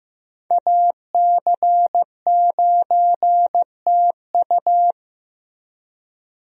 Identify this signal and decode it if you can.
Morse code, heard as a keyed tone, about 15 wpm, 700 Hz. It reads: AC9TU
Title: USA Call Signs - 5 Characters - Repeated 2x - 15wpm